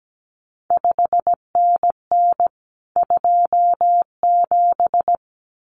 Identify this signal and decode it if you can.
Morse code, heard as a keyed tone, about 17 wpm, 700 Hz. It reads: 5NN 27
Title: CQ WW DX Contest Exchange Encore - 17wpm